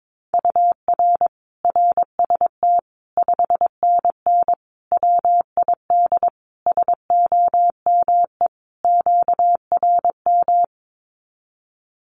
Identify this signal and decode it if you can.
Morse code, heard as a keyed tone, about 22 wpm, 700 Hz. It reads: UR RST 5NN WID SOME QRM
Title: Common QSO Phrases 22wpm